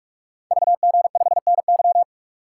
Morse code, heard as a keyed tone, about 44 wpm, 700 Hz. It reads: VG5NY
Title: Code Groups - 5 Letter:Numbers - 45wpm